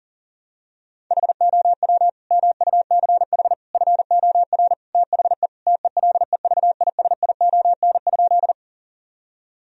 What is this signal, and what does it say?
Morse code, heard as a keyed tone, about 40 wpm, 700 Hz. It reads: HOW MUCH FOR THE TELEVISION?
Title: Sentences from Top 1000 Words - Review Part 4 - 40wpm